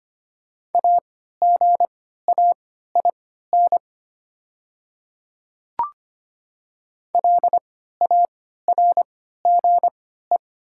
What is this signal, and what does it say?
Morse code, heard as a keyed tone, about 25 wpm, spaced as for 14 wpm, 700 Hz. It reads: AGAIN LARGE
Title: 3x Character Spacing - Top 200 Words - 25wpm